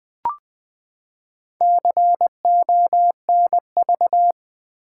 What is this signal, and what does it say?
Morse code, heard as a keyed tone, about 20 wpm, 700 Hz. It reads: CONV
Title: Build A Word - 4 Letters At A Time - 12-Letter Words - 20wpm